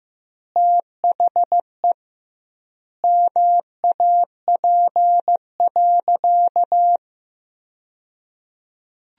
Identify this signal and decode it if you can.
Morse code, heard as a keyed tone, about 15 wpm, 700 Hz. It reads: THE MAP.
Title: 2x Word-Spacing - Sentences from Top 400 Words - 15wpm